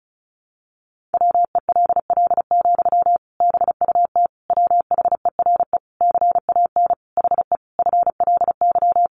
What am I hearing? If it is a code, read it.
Morse code, heard as a keyed tone, about 34 wpm, 700 Hz. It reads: WELL, BUT WHERE CAN HE FLY
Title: Sentences from Top 500 Words 35wpm